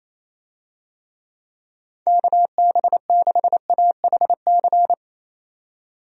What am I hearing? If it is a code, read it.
Morse code, heard as a keyed tone, about 28 wpm, 700 Hz. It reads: KB6AHC
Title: USA Call Signs - 6 Characters - 28wpm